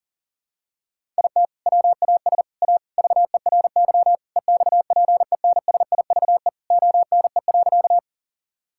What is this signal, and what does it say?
Morse code, heard as a keyed tone, about 40 wpm, 700 Hz. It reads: IT WAS A VERY EXPENSIVE ONE.
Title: Sentences from Top 1000 Words - Review Part 6 - 40wpm